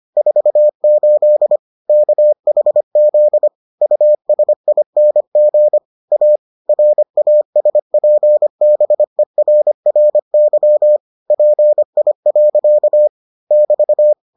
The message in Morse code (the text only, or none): 48 KHZ USING A RASPBERRY PI. <BT>